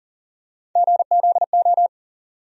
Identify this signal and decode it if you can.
Morse code, heard as a keyed tone, about 40 wpm, 700 Hz. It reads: GZO
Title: USA Call Sign Postfixes - 3 Letters 40wpm